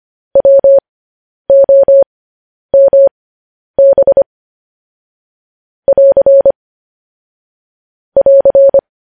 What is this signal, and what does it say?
Morse code, heard as a keyed tone, about 25 wpm, spaced as for 10 wpm, 550 Hz. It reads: WOMB <AR> <AR>
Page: news headlines in morse code at 10 wpm